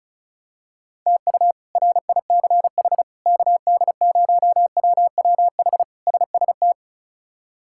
Morse code, heard as a keyed tone, about 34 wpm, 700 Hz. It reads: TU RICH KD0WWH SST
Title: K1USN SST QSO Practice - 35wpm